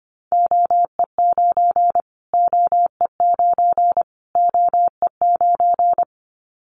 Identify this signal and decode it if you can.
Morse code, heard as a keyed tone, about 25 wpm, 700 Hz. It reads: OE9 OE9 OE9
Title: Callsign Prefixes for Top 25 DXCC - Repeated 3x - 25wpm